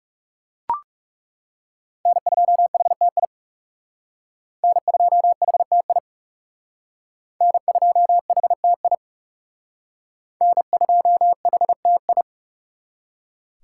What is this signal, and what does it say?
Morse code, heard as a keyed tone, about 37 wpm, 700 Hz. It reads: N2HTI N2HTI N2HTI N2HTI
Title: Speed Racing - USA Call Signs 30wpm